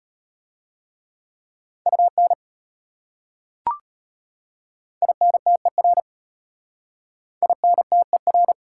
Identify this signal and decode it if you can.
Morse code, heard as a keyed tone, about 33 wpm, 700 Hz. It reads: UN INTER INTER